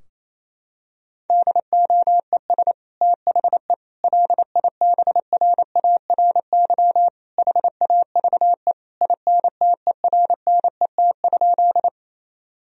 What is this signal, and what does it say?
Morse code, heard as a keyed tone, about 27 wpm, 700 Hz. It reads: DOES THE LIBRARY HAVE INTERNET?
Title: Sentences from Top 1000 Words - Review Part 4 - 28wpm